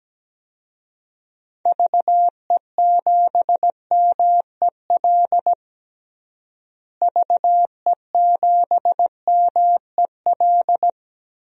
Morse code, heard as a keyed tone, about 17 wpm, 700 Hz. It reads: VE7MEL VE7MEL
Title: DXCC-100 Call Signs - 6 Characters - Repeated 3x - 17wpm